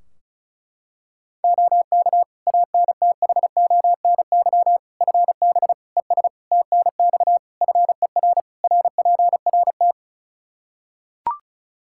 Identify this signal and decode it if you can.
Morse code, heard as a keyed tone, about 36 wpm, 700 Hz. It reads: OK ANTHONY FB ES TNX FER RPRT
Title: Practice QSO - Exchange 2 - Info - 35wpm